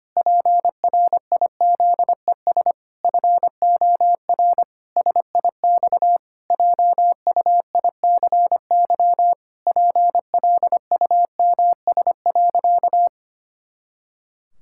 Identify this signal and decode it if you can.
Morse code, heard as a keyed tone, about 25 wpm, 700 Hz. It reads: PRIZES FOR SIX JUICY PLUMS.
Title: Pangram Challenge v2 - 25wpm